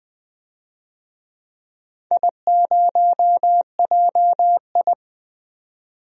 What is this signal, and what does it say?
Morse code, heard as a keyed tone, about 21 wpm, 700 Hz. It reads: I0JI
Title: DXCC-100 Call Signs - 4 Characters - 20wpm